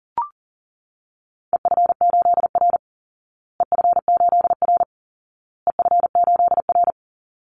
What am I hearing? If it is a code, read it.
Morse code, heard as a keyed tone, about 40 wpm, 700 Hz. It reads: EF8R EF8R EF8R
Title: Call Signs Repeated 2x - 40wpm